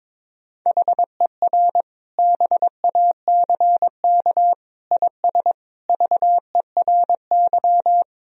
Morse code, heard as a keyed tone, about 22 wpm, 700 Hz. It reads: HER BACK IS VERY
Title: Sentences from Top 1000 Words - Review Part 1 - 22wpm